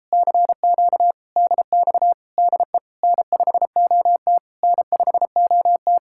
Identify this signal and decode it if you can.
Morse code, heard as a keyed tone, about 37 wpm, 700 Hz. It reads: CQ DX DE N5OT N5OT